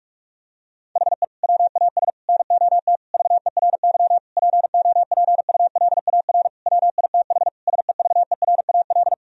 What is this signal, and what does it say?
Morse code, heard as a keyed tone, about 46 wpm, 700 Hz. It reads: HE WAS NOT VERY POPULAR WITH SEVERAL
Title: Sentences from Top 1000 Words - Review Part 3 - 45wpm